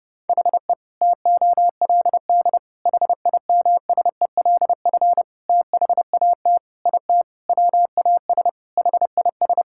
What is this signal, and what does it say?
Morse code, heard as a keyed tone, about 30 wpm, 700 Hz. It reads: HE TOLD HIMSELF THAT IT WAS HIS
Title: Sentences from Top 1600 Words 30wpm